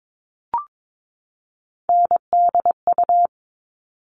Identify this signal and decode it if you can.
Morse code, heard as a keyed tone, about 22 wpm, 700 Hz. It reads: NDU
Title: All Cut Numbers - 3-Digit 22wpm